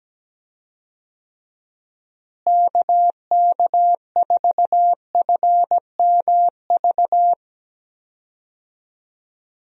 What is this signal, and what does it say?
Morse code, heard as a keyed tone, about 17 wpm, 700 Hz. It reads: KK4FMV
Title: USA Call Signs - 6 Characters - Repeated 2x - 17wpm